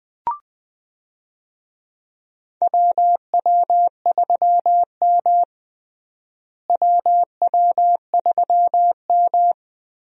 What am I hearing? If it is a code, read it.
Morse code, heard as a keyed tone, about 20 wpm, 700 Hz. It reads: WW3M WW3M
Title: USA Call Signs - 4 Characters - Repeated 2x - 20wpm